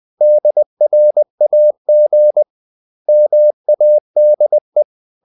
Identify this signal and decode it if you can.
Morse code, heard as a keyed tone, about 20 wpm, 600 Hz. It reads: DRAG MADE
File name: the old man and the sea part 2 20wpm